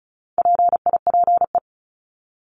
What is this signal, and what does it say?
Morse code, heard as a keyed tone, about 34 wpm, 700 Hz. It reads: PIPE
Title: Top 1251-1500 Words 35wpm